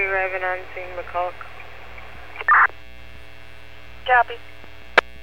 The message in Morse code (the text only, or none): none